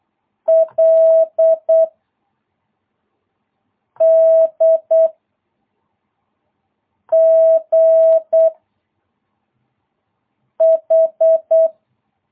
Morse code, heard as a keyed tone, about 8 wpm, 660 Hz. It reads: L D G H
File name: jh8cbh / 2022cw